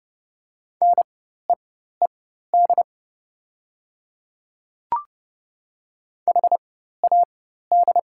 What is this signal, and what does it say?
Morse code, heard as a keyed tone, about 30 wpm, spaced as for 14 wpm, 690 Hz. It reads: NEED HAD
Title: 4x Character Spacing - Top 200 Words - 30wpm